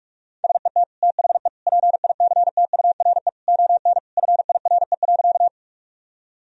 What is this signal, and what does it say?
Morse code, heard as a keyed tone, about 46 wpm, 700 Hz. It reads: SET THE PICTURE ON FIRE.
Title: Sentences from Top 1000 Words - Review Part 1 - 45wpm